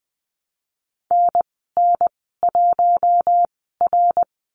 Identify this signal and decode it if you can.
Morse code, heard as a keyed tone, about 20 wpm, spaced as for 14 wpm, 700 Hz. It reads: NN1R
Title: 069 Lesson 12g - Letter d - Call Signs 20wpm